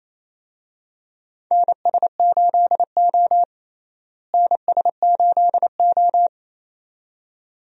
Morse code, heard as a keyed tone, about 28 wpm, 700 Hz. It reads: NS8O NS8O
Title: USA Call Signs - 4 Characters - Repeated 2x - 28wpm